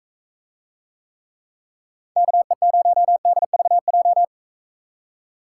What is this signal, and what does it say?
Morse code, heard as a keyed tone, about 44 wpm, 700 Hz. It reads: KE0DVJ